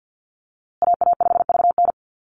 Morse code, heard as a keyed tone, about 48 wpm, 700 Hz. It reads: AA5VN
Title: USA Call Signs Repeated 3x 50wpm